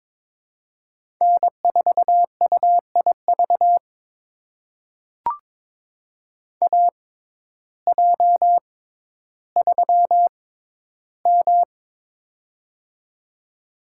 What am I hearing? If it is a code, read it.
Morse code, heard as a keyed tone, about 22 wpm, spaced as for 12 wpm, 700 Hz. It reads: N4UIV AJ3M
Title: USA Call Signs - 6x Character Spacing - 22wpm